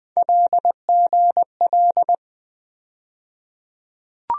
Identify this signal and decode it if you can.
Morse code, heard as a keyed tone, about 20 wpm, 700 Hz. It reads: LGL
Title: USA Call Sign Postfixes - 3 Letters 20wpm